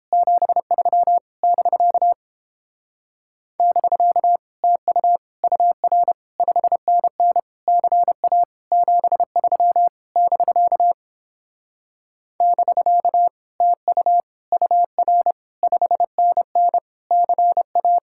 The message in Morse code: 73 <BK> <BK> TU UR 5NN CA 73 <BK> <BK> TU UR 5NN CA

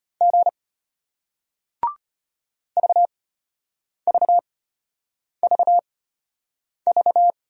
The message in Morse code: G VVVV

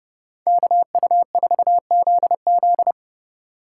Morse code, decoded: KU4ZZ

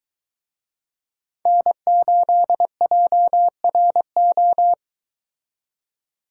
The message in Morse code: N8JRO

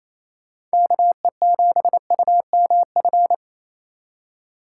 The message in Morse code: KE7UMF

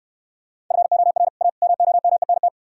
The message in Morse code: FOR A MOMENT

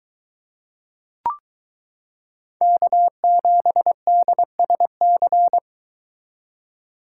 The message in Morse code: K7DSC

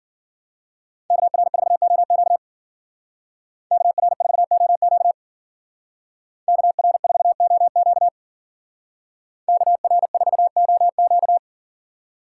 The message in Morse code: KR4OQ KR4OQ KR4OQ KR4OQ

